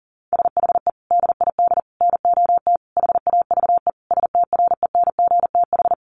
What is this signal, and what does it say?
Morse code, heard as a keyed tone, about 40 wpm, 700 Hz. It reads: SHE DID NOT HAVE STRENGTH
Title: Sentences from Top 1300 Words 40wpm